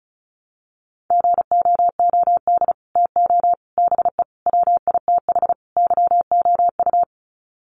Morse code, heard as a keyed tone, about 34 wpm, 700 Hz. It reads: GOOD TO BE WITH YOU
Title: Sets of 5 Words Encore2 35wpm